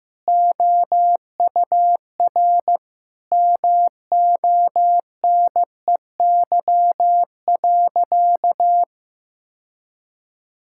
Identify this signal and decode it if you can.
Morse code, heard as a keyed tone, about 15 wpm, 700 Hz. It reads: OUR MONEY.
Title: Sentences from Top 1600 Words 15wpm